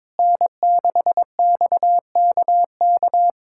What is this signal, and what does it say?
Morse code, heard as a keyed tone, about 22 wpm, 700 Hz. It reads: N6XKK
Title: USA Callsign DE Callsign - 5 Characters - 22wpm